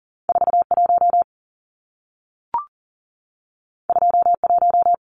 Morse code, heard as a keyed tone, about 40 wpm, 700 Hz. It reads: 41 21
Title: Numbers - 2-Digit 40wpm